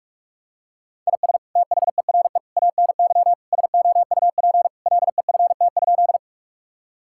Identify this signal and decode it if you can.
Morse code, heard as a keyed tone, about 46 wpm, 700 Hz. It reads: IS THERE ANY SOUP LEFT?
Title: Sentences from Top 1000 Words - Review Part 6 - 45wpm